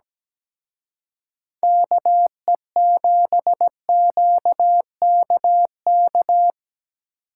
Morse code, heard as a keyed tone, about 20 wpm, 700 Hz. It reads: KE7QKK
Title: Speed Racing - USA 2x3 Call Signs 17wpm